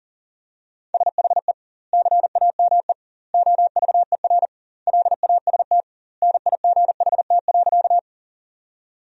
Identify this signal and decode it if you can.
Morse code, heard as a keyed tone, about 40 wpm, 700 Hz. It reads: SHE CAME OVER LAST NIGHT.
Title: 2x Word-Spacing - Sentences from Top 300 Words - 40wpm